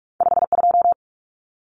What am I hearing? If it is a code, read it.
Morse code, heard as a keyed tone, about 44 wpm, 700 Hz. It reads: 52
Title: Numbers - 2-Digit 45wpm